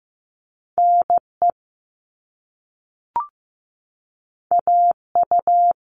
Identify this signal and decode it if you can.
Morse code, heard as a keyed tone, about 15 wpm, 700 Hz. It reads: NE AU
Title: All Cut Numbers - 2-Digit 15wpm